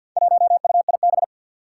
Morse code, heard as a keyed tone, about 52 wpm, 700 Hz. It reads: 1UIB
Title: Four Challenging Letter-Number Combinations 50wpm